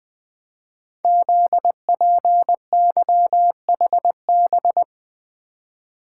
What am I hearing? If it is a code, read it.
Morse code, heard as a keyed tone, about 20 wpm, 700 Hz. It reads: ZPYHB
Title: Code Groups - 5 Letter:Numbers - 20wpm